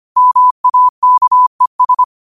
Morse code, heard as a keyed tone, about 25 wpm, 1000 Hz. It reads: MAKES
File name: qotdfiles / 25wpm